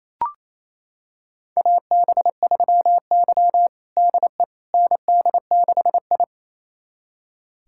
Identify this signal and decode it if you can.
Morse code, heard as a keyed tone, about 28 wpm, 700 Hz. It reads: AB3Y DE ND6I
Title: USA Callsign DE Callsign - 4 Characters - 28wpm